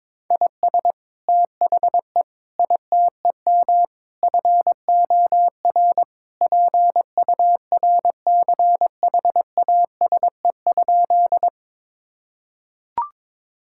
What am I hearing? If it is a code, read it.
Morse code, heard as a keyed tone, about 22 wpm, 700 Hz. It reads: IS THE ITEM FOR PURCHASE?
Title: Sentences from Top 1000 Words - Review Part 6 - 22wpm